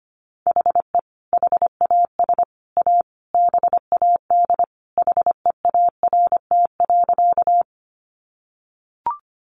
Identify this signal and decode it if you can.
Morse code, heard as a keyed tone, about 25 wpm, 700 Hz. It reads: HE HAS A BAD HEART.
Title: Sentences from Top 1200 Words 25wpm